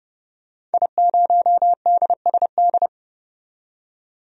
Morse code, heard as a keyed tone, about 30 wpm, 700 Hz. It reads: I0DSD
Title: DXCC-100 Call Signs - 5 Characters - 30wpm